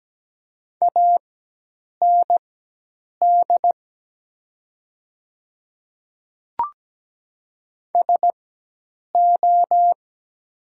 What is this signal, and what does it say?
Morse code, heard as a keyed tone, about 17 wpm, spaced as for 8 wpm, 700 Hz. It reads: AND SO
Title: 4x Character Spacing - Top 300 Words - 17wpm